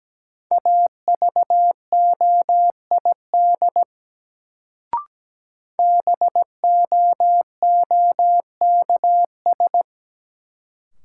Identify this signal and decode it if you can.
Morse code, heard as a keyed tone, about 17 wpm, 700 Hz. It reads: AVOID BOOKS
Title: Sentences from Top 1200 Words - New Word Drill Alt - 17wpm